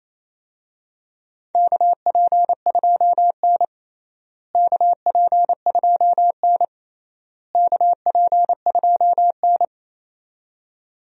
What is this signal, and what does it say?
Morse code, heard as a keyed tone, about 28 wpm, 700 Hz. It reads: KP2N KP2N KP2N
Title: DXCC-100 Call Signs - 4 Characters - Repeated 3x - 28wpm